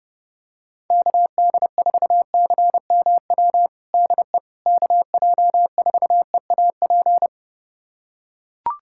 Morse code, heard as a keyed tone, about 30 wpm, 700 Hz. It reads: KD4CMW DE KJ4EAP
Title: USA Callsign DE Callsign 30wpm